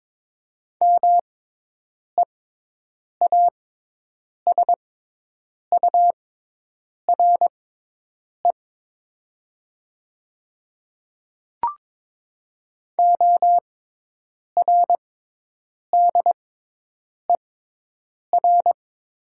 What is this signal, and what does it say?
Morse code, heard as a keyed tone, about 22 wpm, spaced as for 8 wpm, 700 Hz. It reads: MEASURE ORDER
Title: 6x Character Spacing - Top 500 Words - 22wpm